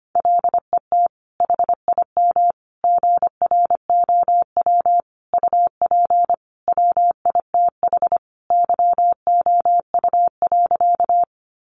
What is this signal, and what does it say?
Morse code, heard as a keyed tone, about 25 wpm, 700 Hz. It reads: LET HIM GROW UP WITH YOU.
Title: Sentences from Top 300 Words 25wpm